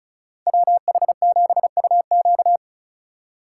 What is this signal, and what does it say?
Morse code, heard as a keyed tone, about 34 wpm, 700 Hz. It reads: WH7UQ
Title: USA Call Signs - 5 Characters - 2x Character Spacing - 35wpm